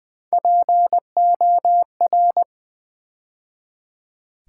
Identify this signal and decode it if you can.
Morse code, heard as a keyed tone, about 20 wpm, 700 Hz. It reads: POR
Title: Build A Word - 3 Letters At A Time - 8-Letter Words - 20wpm